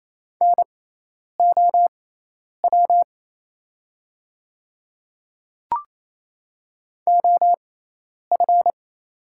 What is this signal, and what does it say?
Morse code, heard as a keyed tone, about 28 wpm, spaced as for 10 wpm, 700 Hz. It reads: NOW OF
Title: 6x Character Spacing - Top 100 Words - 28wpm